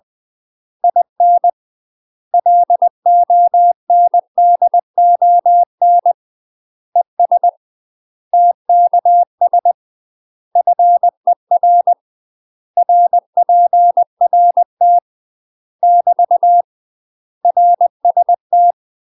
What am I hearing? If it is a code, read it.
Morse code, heard as a keyed tone, about 20 wpm, 700 Hz. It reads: IN LONDON ES TKS FER RPRT <BT> RST
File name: UK MASTER QSOs TRAINING / 20WPM / UK MASTER QSOs 0045